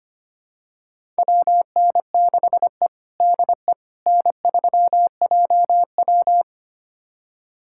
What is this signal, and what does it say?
Morse code, heard as a keyed tone, about 25 wpm, 700 Hz. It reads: WN6E DE N3JW
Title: USA Callsign DE Callsign - 4 Characters - 25wpm